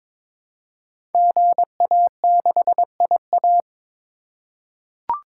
Morse code, heard as a keyed tone, about 23 wpm, 700 Hz. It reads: GA6IA
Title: Code Groups - 5 Letter:Numbers - 22wpm